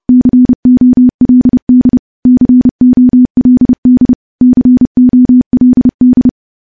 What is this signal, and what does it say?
Morse code, heard as a keyed tone, about 30 wpm, 250 Hz. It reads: COLD COLD COLD
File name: morse / morse-training-30WPM